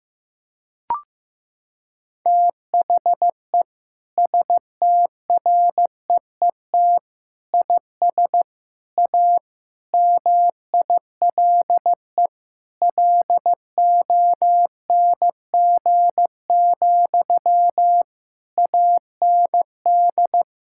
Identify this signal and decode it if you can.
Morse code, heard as a keyed tone, about 15 wpm, 700 Hz. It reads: THE STREET IS A MILE LONG, AND
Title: Sentences from Top 1600 Words 15wpm